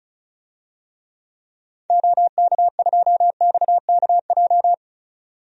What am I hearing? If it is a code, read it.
Morse code, heard as a keyed tone, about 36 wpm, 700 Hz. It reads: OK2XKJ
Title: DXCC-100 Call Signs - 6 Characters - 35wpm